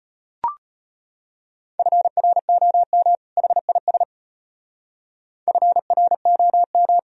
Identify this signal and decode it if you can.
Morse code, heard as a keyed tone, about 30 wpm, 700 Hz. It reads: FROM HIS FROM
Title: Speed-Racing - Sets of 2 Words 25wpm